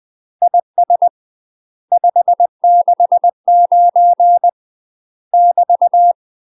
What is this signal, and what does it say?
Morse code, heard as a keyed tone, about 20 wpm, 700 Hz. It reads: IS 569 <BT>
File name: UK MASTER QSOs TRAINING / 20WPM / UK MASTER QSOs 0053